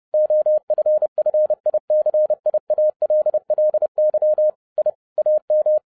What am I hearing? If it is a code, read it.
Morse code, heard as a keyed tone, about 30 wpm, 610 Hz. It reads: OFFICIALLY I AM
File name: 2011-07-mp3 / n40 EXCITING QRP CW OPERATION 30wpm